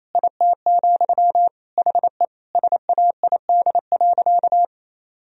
Morse code, heard as a keyed tone, about 28 wpm, 700 Hz. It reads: IT, HE SAID.
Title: Sentences from Top 1000 Words - Review Part 3 - 28wpm